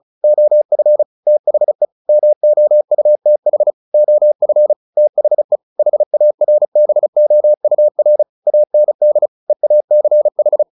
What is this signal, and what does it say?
Morse code, heard as a keyed tone, about 34 wpm, 600 Hz. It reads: OF THE MOUTH OF THE HARBOUR AND EACH
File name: the old man and the sea / the old man and the sea 1 35wpm